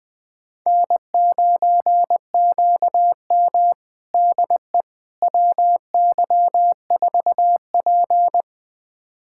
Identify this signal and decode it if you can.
Morse code, heard as a keyed tone, about 20 wpm, 700 Hz. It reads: N9QM DE WY4P
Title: USA Callsign DE Callsign - 4 Characters - 20wpm